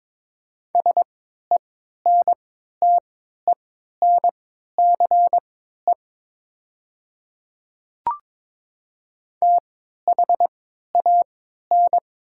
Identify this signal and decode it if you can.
Morse code, heard as a keyed tone, about 22 wpm, spaced as for 12 wpm, 700 Hz. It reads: SENTENCE THAN
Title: 3x Character Spacing - Top 300 Words - 22wpm